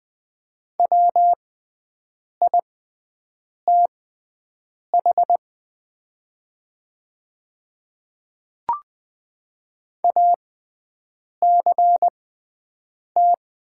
Morse code, heard as a keyed tone, about 20 wpm, spaced as for 7 wpm, 700 Hz. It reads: WITH ACT
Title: 6x Character Spacing - Top 200 Words - 20wpm